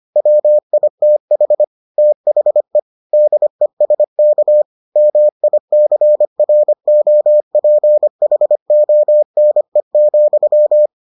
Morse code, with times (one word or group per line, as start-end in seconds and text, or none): 0.2-1.6 WITH
2.0-2.8 THE
3.1-4.6 DESK
5.0-10.9 MICROPHONE,